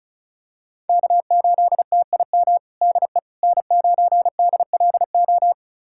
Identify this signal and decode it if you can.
Morse code, heard as a keyed tone, about 34 wpm, 700 Hz. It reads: K8TIM DE N9DLO
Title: USA Callsign DE Callsign - 5 Characters - 35wpm